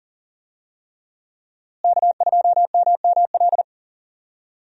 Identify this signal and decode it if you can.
Morse code, heard as a keyed tone, about 40 wpm, 700 Hz. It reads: K2MML
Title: USA Call Signs - 5 Characters - Repeated 3x - 40wpm